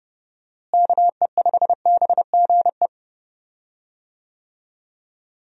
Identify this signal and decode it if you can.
Morse code, heard as a keyed tone, about 32 wpm, 700 Hz. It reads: KE5BGE
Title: USA Call Signs - 6 Characters - Repeated 2x - 30wpm